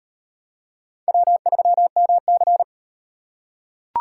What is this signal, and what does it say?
Morse code, heard as a keyed tone, about 38 wpm, 700 Hz. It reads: W3MC